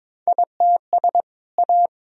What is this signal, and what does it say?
Morse code, heard as a keyed tone, about 22 wpm, 700 Hz. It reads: ITS A
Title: Sentences from Top 1000 Words - Review Part 3 - 22wpm